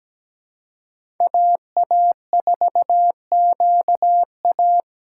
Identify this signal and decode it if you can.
Morse code, heard as a keyed tone, about 17 wpm, 700 Hz. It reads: AA4QA
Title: USA Call Signs - 5 Characters - 2x Character Spacing - 17wpm